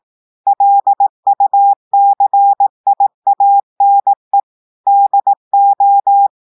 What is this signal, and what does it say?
Morse code, heard as a keyed tone, about 18 wpm, 800 Hz. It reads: LUCIANE DO
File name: audios / portugais / portugais texte en clair 1 vitesse 18 mots minutes